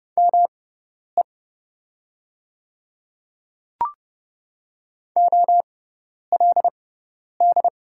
Morse code, heard as a keyed tone, about 30 wpm, spaced as for 10 wpm, 690 Hz. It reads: ME OLD
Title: 6x Character Spacing - Top 200 Words - 30wpm